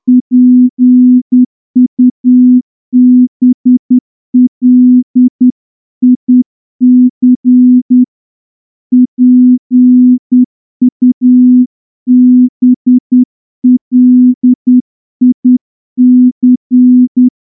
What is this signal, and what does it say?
Morse code, heard as a keyed tone, about 10 wpm, 250 Hz. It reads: PUBLIC PUBLIC
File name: morse-training-10WPM